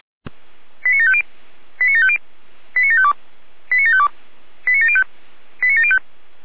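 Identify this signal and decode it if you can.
Five-tone selective call (ZVEI), heard as quick runs of five tones, tones of about 70 ms. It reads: 89750 89750 89752 89752 89705 89705
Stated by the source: ZVEI five-tone selective call received on SDR.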